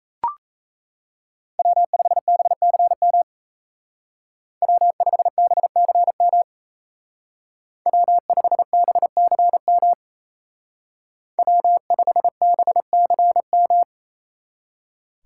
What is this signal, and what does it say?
Morse code, heard as a keyed tone, about 34 wpm, 700 Hz. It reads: W5BCM W5BCM W5BCM W5BCM
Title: Speed Racing - USA Call Signs 28wpm